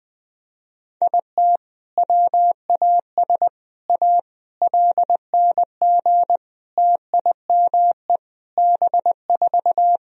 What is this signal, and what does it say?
Morse code, heard as a keyed tone, about 20 wpm, 700 Hz. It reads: IT WAS A LNG TIME B4
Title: Sentences from Top 500 Words and Abbreviations 20wpm